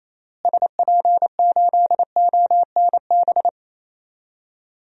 Morse code, heard as a keyed tone, about 28 wpm, 700 Hz. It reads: SP8ONB
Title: DXCC-100 Call Signs - 6 Characters - Repeated 3x - 28wpm